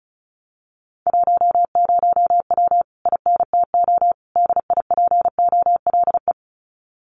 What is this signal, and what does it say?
Morse code, heard as a keyed tone, about 34 wpm, 700 Hz. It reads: 10W INTO DIPOLE
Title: Common QSO Phrases Encore 35wpm